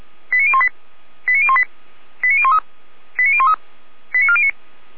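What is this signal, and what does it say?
Five-tone selective call (ZVEI), heard as quick runs of five tones, tones of about 70 ms. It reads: 89018 89018 89013 89013 89409
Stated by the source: SDR-received ZVEI five-tone call 89560.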